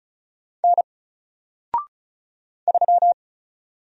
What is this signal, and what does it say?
Morse code, heard as a keyed tone, about 36 wpm, 700 Hz. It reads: N 3
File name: Single-Letter-Number-35wpm